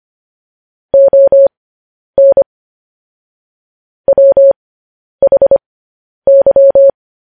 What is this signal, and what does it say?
Morse code, heard as a keyed tone, about 25 wpm, spaced as for 10 wpm, 550 Hz. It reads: ON WHY